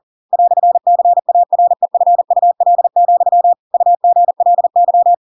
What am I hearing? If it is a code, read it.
Morse code, heard as a keyed tone, about 40 wpm, 700 Hz. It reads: "CAREFUL, UGLY
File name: Pinocchio / Pinocchio ch4 40 40 ik1hge